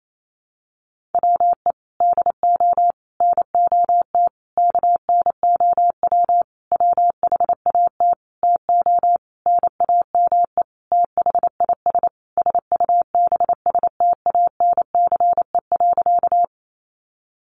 Morse code, heard as a keyed tone, about 28 wpm, 700 Hz. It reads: WE DO NOT KNOW WHAT TO NAME THIS SUBSTANCE.
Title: Sentences from Top 1500 Words 28wpm